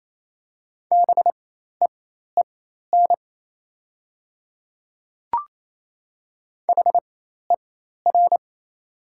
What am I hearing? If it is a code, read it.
Morse code, heard as a keyed tone, about 28 wpm, spaced as for 13 wpm, 700 Hz. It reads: BEEN HER
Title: 4x Character Spacing - Top 100 Words - 28wpm